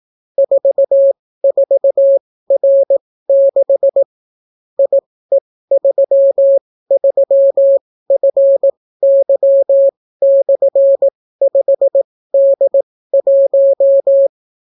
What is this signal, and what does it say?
Morse code, heard as a keyed tone, about 18 wpm, spaced as for 14 wpm, 550 Hz. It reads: 44R6 IE33FY/5D1